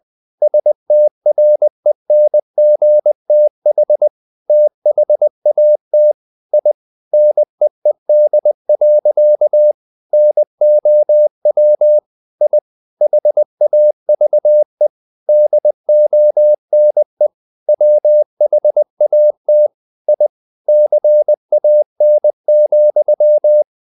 Morse code, heard as a keyed tone, about 20 wpm, 600 Hz. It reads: STRENGTH THAT I NEED. NOW I HAVE DONE WHAT I CAN,
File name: the old man and the sea part 2 20wpm